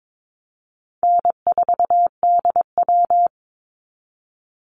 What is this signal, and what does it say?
Morse code, heard as a keyed tone, about 22 wpm, 700 Hz. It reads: N4DW
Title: Easy Call Signs Repeated 2x - 22wpm